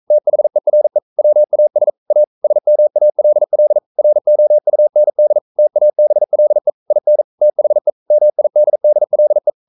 Morse code, heard as a keyed tone, about 42 wpm, 600 Hz. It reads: THERE WAS A SMALL ROUND TABLE IN THE MIDDLE
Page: Romanen Never af Ken Follet på engelsk.